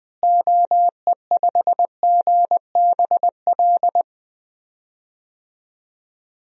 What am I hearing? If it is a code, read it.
Morse code, heard as a keyed tone, about 21 wpm, 700 Hz. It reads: OE5GBL
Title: DXCC-100 Call Signs - 6 Characters - Repeated 2x - 20wpm